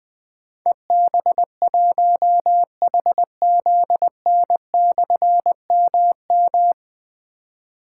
Morse code, heard as a keyed tone, about 20 wpm, 700 Hz. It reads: EB1HZN/MM
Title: Difficult Call Signs 20wpm